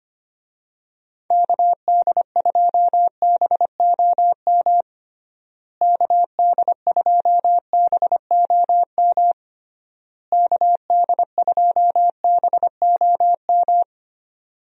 Morse code, heard as a keyed tone, about 25 wpm, 700 Hz. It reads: KD2BOM KD2BOM KD2BOM
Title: USA Call Signs - 6 Characters - Repeated 3x - 25wpm